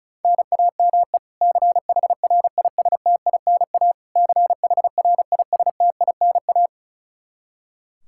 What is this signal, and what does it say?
Morse code, heard as a keyed tone, about 36 wpm, 700 Hz. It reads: NAME CHRISTINA CHRISTINA
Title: K1USN SST QSO Practice 35wpm